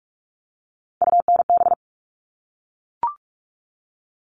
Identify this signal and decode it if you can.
Morse code, heard as a keyed tone, about 44 wpm, 700 Hz. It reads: UNB
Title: All Cut Numbers - 3-Digit 45wpm